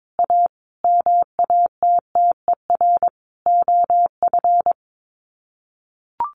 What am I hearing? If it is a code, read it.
Morse code, heard as a keyed tone, about 22 wpm, 700 Hz. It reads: A MATTER OF
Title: Sets of 5 Words Encore2 22wpm